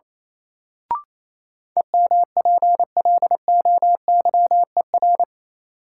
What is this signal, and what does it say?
Morse code, heard as a keyed tone, about 28 wpm, 700 Hz. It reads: EMPLOYER
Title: Sentences from Top 1600 Words - New Word Drill - 28wpm